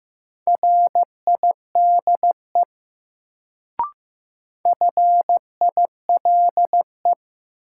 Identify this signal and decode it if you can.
Morse code, heard as a keyed tone, about 15 wpm, 700 Hz. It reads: RIDE FILE
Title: Sentences from Top 1200 Words - New Word Drill Alt - 15wpm